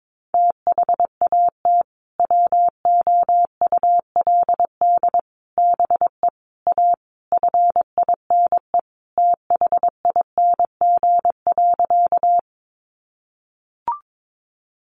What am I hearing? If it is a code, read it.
Morse code, heard as a keyed tone, about 22 wpm, 700 Hz. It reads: THAT WOULD BE A FINE THING.
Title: Sentences from Top 500 Words 22wpm